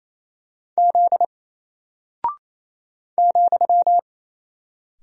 Morse code, heard as a keyed tone, about 28 wpm, 700 Hz. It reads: Z ,
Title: Single Character (Letter:Number:Punctuation:Prosign) Alt 28wpm